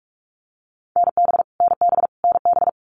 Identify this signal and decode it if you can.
Morse code, heard as a keyed tone, about 44 wpm, 700 Hz. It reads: NB NB NB
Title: Canadian Provinces and Territories - Repeated 3x - 45wpm